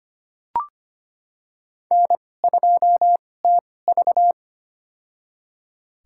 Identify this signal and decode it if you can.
Morse code, heard as a keyed tone, about 25 wpm, spaced as for 18 wpm, 700 Hz. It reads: N2TV
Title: USA Call Signs - 4 Characters - 2x Character Spacing - 25wpm